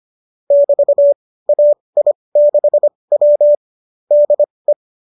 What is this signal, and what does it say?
Morse code, heard as a keyed tone, about 25 wpm, spaced as for 20 wpm, 590 Hz. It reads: <BT> AI6W DE